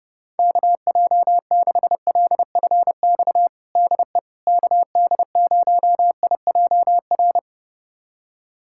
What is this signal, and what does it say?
Morse code, heard as a keyed tone, about 30 wpm, 700 Hz. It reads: KJ6LFX DE KD0IJR
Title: USA Callsign DE Callsign - 6 Characters - 30wpm